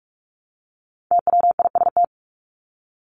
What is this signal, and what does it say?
Morse code, heard as a keyed tone, about 44 wpm, 700 Hz. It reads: TWIST
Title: Top 1751-2000 Words 45wpm